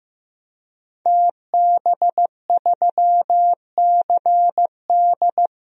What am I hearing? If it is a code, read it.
Morse code, heard as a keyed tone, about 15 wpm, 700 Hz. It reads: TB3CD
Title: DXCC-100 Call Signs - 5 Characters - 15wpm